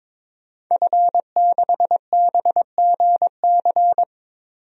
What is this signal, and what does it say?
Morse code, heard as a keyed tone, about 22 wpm, 700 Hz. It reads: F6BGC
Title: DXCC-100 Call Signs - 5 Characters - 22wpm